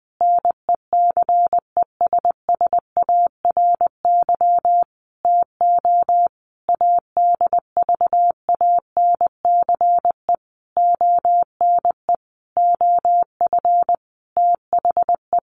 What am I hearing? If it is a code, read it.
Morse code, heard as a keyed tone, about 20 wpm, 700 Hz. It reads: NECESSARY TO ADVANCE ONE OF THE
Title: Sentences from Top 1400 Words 20wpm